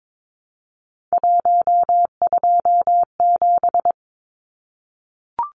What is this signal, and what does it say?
Morse code, heard as a keyed tone, about 22 wpm, 700 Hz. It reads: 127
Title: Numbers - 3-Digit 22wpm